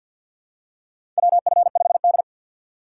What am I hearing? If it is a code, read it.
Morse code, heard as a keyed tone, about 52 wpm, 700 Hz. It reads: WF5D
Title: USA Call Signs - 4 Characters - 2x Character Spacing - 50wpm